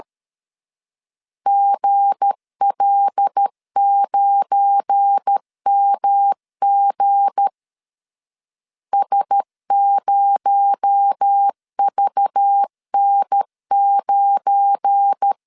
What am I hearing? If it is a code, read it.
Morse code, heard as a keyed tone, about 13 wpm, 800 Hz. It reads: GL9MG S0VN9
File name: LEZIONI / 7 GF9Z / 7 GF9 12wpm 2319